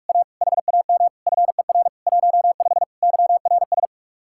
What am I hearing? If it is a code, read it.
Morse code, heard as a keyed tone, about 49 wpm, 700 Hz. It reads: A HAM FER 15 YRS